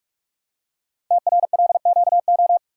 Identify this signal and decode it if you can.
Morse code, heard as a keyed tone, about 44 wpm, 700 Hz. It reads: TRLQO
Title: Code Groups - 5 Letter:Numbers - 45wpm